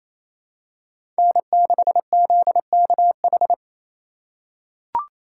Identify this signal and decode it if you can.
Morse code, heard as a keyed tone, about 28 wpm, 700 Hz. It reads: N6ZKH
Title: USA Call Signs - 3x Character Spacing - 28wpm